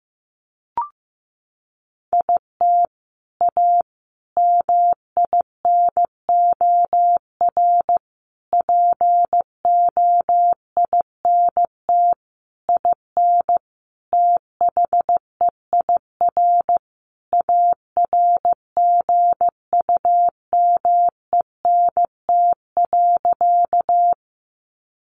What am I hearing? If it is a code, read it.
Morse code, heard as a keyed tone, about 15 wpm, 700 Hz. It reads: IT A MINOR POINT IN THEIR ARGUMENT.
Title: Sentences from Top 1500 Words 15wpm